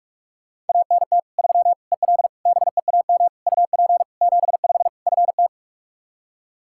Morse code, heard as a keyed tone, about 41 wpm, 700 Hz. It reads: ANT 3 EL BEAM UP 75 FT
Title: Practice QSO - Exchange 2 - Info - 45wpm